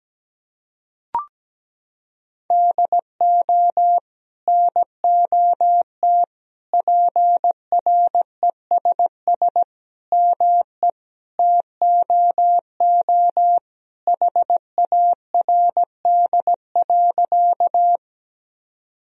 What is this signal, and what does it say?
Morse code, heard as a keyed tone, about 17 wpm, 700 Hz. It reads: DO NOT PRESS ME TOO HARD.
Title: Sentences from Top 1000 Words - Review Part 1 - 17wpm